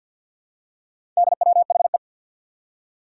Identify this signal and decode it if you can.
Morse code, heard as a keyed tone, about 52 wpm, 700 Hz. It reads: DWHE
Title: Four Challenging Letter-Number Combinations 50wpm